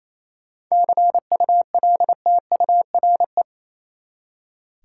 Morse code, heard as a keyed tone, about 28 wpm, 700 Hz. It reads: CULTURE
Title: Seven Letter Words - 28wpm